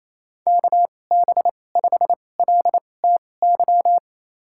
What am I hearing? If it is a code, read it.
Morse code, heard as a keyed tone, about 28 wpm, spaced as for 20 wpm, 700 Hz. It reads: KB5LTY
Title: USA Call Signs - 6 Characters - 2x Character Spacing - 28wpm